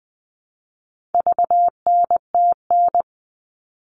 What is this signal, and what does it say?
Morse code, heard as a keyed tone, about 20 wpm, 700 Hz. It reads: VNTN
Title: All Cut Numbers - 4-Digit 20wpm